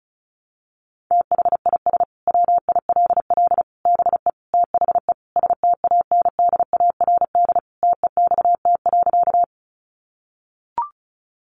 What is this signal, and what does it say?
Morse code, heard as a keyed tone, about 34 wpm, 700 Hz. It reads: THIS WILL BE THE STANDARD TEXT.
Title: Sentences from Top 1200 Words 35wpm